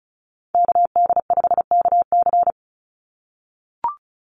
Morse code, heard as a keyed tone, about 34 wpm, 700 Hz. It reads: KD5KC
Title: NA SOTA Activator Callsigns 35wpm